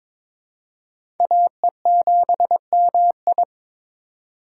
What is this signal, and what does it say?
Morse code, heard as a keyed tone, about 21 wpm, 700 Hz. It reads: AE7MI
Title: USA Call Signs - 5 Characters - 4x Character Spacing - 22wpm